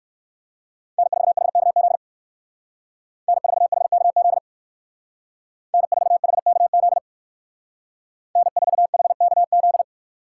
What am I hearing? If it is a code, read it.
Morse code, heard as a keyed tone, about 56 wpm, 700 Hz. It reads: N4HKZ N4HKZ N4HKZ N4HKZ